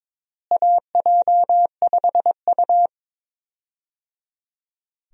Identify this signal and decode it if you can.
Morse code, heard as a keyed tone, about 22 wpm, 700 Hz. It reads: AJ5U
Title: USA Call Signs - 4 Characters - Repeated 3x - 22wpm